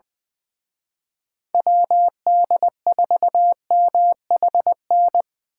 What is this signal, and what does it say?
Morse code, heard as a keyed tone, about 24 wpm, 700 Hz. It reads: WD4MHN
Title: Speed Racing - USA 2x3 Call Signs 20wpm